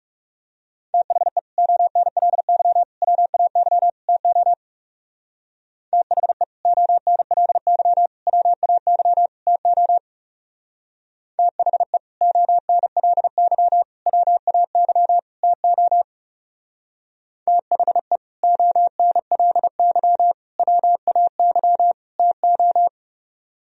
Morse code, heard as a keyed tone, about 37 wpm, 690 Hz. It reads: THE ONLY WAY TO THE ONLY WAY TO THE ONLY WAY TO THE ONLY WAY TO
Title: Speed-Racing - Sets of 4 Words 30wpm